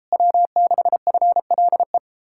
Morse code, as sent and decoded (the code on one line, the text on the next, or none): .-- -.... ..-. .-.. .
W6FLE